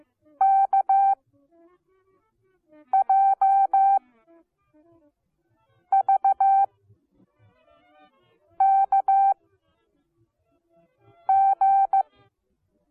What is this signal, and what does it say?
Morse code, heard as a keyed tone, about 15 wpm, spaced as for 4 wpm, 780 Hz. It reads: KJVKG